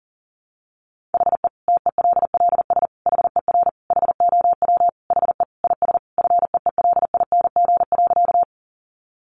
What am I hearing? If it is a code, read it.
Morse code, heard as a keyed tone, about 40 wpm, 700 Hz. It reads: HE TELLS HER HOW HE IS FEELING.
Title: Additional Easy Phrases 40wpm